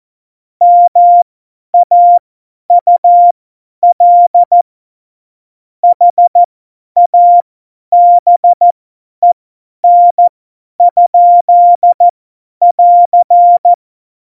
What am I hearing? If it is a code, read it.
Morse code, heard as a keyed tone, about 14 wpm, spaced as for 10 wpm, 700 Hz. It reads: MAUL HABEN? <AR>
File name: Morsekurs / Witze / Witz 03. November 2020, 50 F70